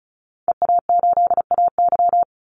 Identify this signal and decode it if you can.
Morse code, heard as a keyed tone, about 34 wpm, 700 Hz. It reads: EA8AY
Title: Call Signs Repeated 3x - 35wpm